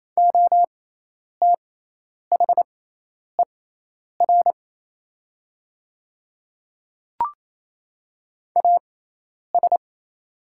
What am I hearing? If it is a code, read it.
Morse code, heard as a keyed tone, about 27 wpm, spaced as for 10 wpm, 690 Hz. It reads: OTHER AS
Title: 6x Character Spacing - Top 100 Words - 28wpm